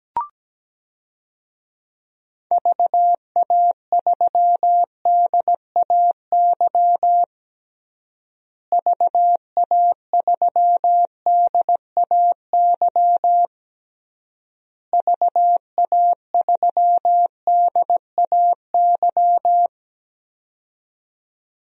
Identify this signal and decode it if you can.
Morse code, heard as a keyed tone, about 17 wpm, 700 Hz. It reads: VA3DAY VA3DAY VA3DAY
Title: DXCC-100 Call Signs - 6 Characters - Repeated 3x - 17wpm